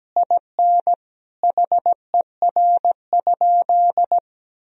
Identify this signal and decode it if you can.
Morse code, heard as a keyed tone, about 17 wpm, 700 Hz. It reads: IN HER?
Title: Sentences from Top 1000 Words - Review Part 4 - 17wpm